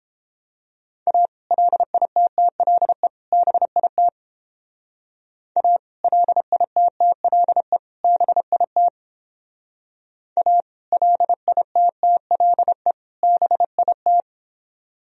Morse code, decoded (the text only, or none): A LITTLE BIT A LITTLE BIT A LITTLE BIT